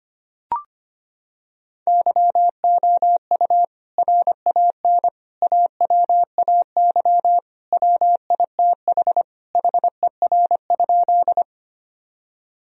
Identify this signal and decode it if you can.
Morse code, heard as a keyed tone, about 25 wpm, 700 Hz. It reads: YOU RAN AWAY WITH HER?
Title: Sentences from Top 1000 Words - Review Part 3 - 25wpm